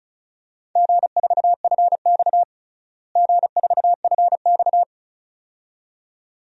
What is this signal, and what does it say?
Morse code, heard as a keyed tone, about 34 wpm, 700 Hz. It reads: G4FX G4FX
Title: DXCC-100 Call Signs - 4 Characters - Repeated 2x - 35wpm